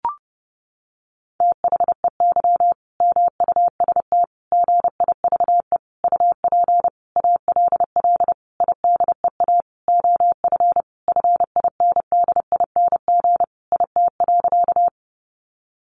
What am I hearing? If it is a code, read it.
Morse code, heard as a keyed tone, about 30 wpm, 700 Hz. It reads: THEY MUST GIVE UP ALL IDEA OF FINDING IT.
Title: Sentences from Top 1100 Words 30wpm